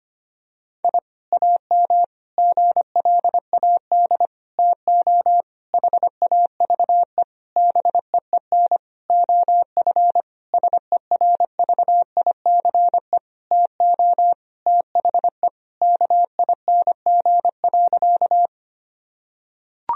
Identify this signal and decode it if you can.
Morse code, heard as a keyed tone, about 25 wpm, 700 Hz. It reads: I AM GLAD TO HAVE BEEN OF SERVICE TO THE KING.
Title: Sentences from Top 1000 Words - Review Part 4 - 25wpm